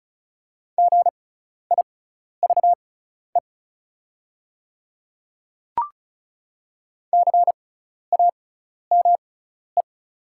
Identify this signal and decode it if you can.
Morse code, heard as a keyed tone, about 36 wpm, spaced as for 12 wpm, 700 Hz. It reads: GIVE CAME
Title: 6x Character Spacing - Top 200 Words - 35wpm